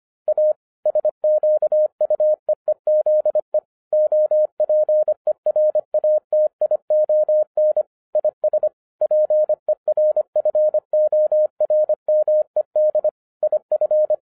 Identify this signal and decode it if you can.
Morse code, heard as a keyed tone, about 25 wpm, 610 Hz. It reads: A SQUEEZE OPERATION IS PERFORMED IF